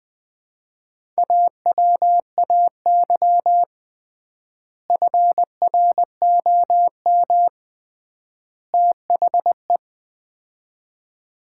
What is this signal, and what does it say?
Morse code, heard as a keyed tone, about 20 wpm, 700 Hz. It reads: AWAY FROM THE
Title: 3x Word-Spacing - Sets of 3 Words - 20wpm